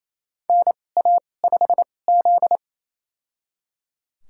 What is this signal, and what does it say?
Morse code, heard as a keyed tone, about 28 wpm, spaced as for 20 wpm, 700 Hz. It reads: NA5Z
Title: USA Call Signs - 2x Character Spacing - 28wpm